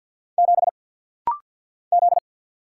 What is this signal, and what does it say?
Morse code, heard as a keyed tone, about 48 wpm, 700 Hz. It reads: 7 Z
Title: Single Letter-Number Alt - Rapid-Fire 50wpm